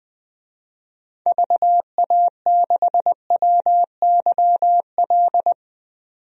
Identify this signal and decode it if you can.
Morse code, heard as a keyed tone, about 20 wpm, 700 Hz. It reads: VA6WYL
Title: DXCC-100 Call Signs - 6 Characters - 20wpm